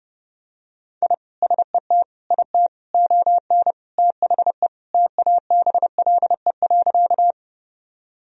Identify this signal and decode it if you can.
Morse code, heard as a keyed tone, about 32 wpm, 700 Hz. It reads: I SET IT ON THE TABLE.
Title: Sentences from Top 1000 Words - Review Part 2 - 30wpm